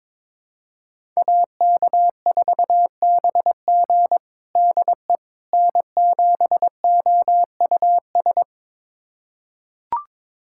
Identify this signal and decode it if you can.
Morse code, heard as a keyed tone, about 22 wpm, 700 Hz. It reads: AK4BG DE N7OUS
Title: USA Callsign DE Callsign - 5 Characters - 22wpm